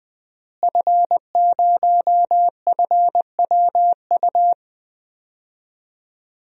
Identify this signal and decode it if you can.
Morse code, heard as a keyed tone, about 20 wpm, 700 Hz. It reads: F0FWU
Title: DXCC-100 Call Signs - 5 Characters - Repeated 2x - 20wpm